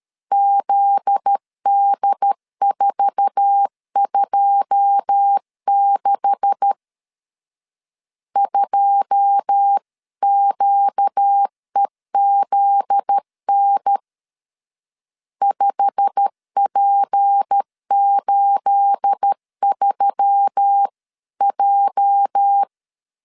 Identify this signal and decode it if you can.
Morse code, heard as a keyed tone, about 13 wpm, 800 Hz. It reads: ZD426 2QEZN 5P83J
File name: CW / DET / 1 9 1500C S A 12wpm 3800